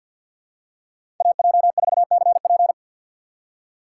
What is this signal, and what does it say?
Morse code, heard as a keyed tone, about 54 wpm, 700 Hz. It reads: AJ4CP
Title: USA Call Signs - 6x Character Spacing - 50wpm